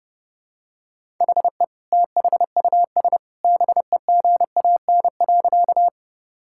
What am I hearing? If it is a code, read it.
Morse code, heard as a keyed tone, about 32 wpm, 700 Hz. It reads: HE THUS BEGAN.
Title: Sentences from Top 1000 Words - Review Part 5 - 30wpm